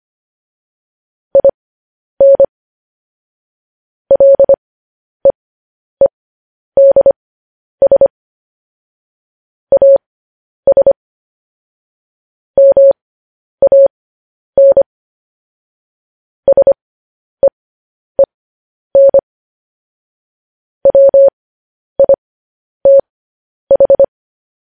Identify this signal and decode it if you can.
Morse code, heard as a keyed tone, about 25 wpm, spaced as for 10 wpm, 550 Hz. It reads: IN LEEDS AS MAN SEEN WITH
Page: news headlines in morse code at 10 wpm